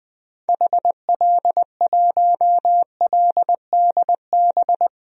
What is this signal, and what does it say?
Morse code, heard as a keyed tone, about 20 wpm, 700 Hz. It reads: HL1LDB
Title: DXCC-100 Call Signs - 6 Characters - 20wpm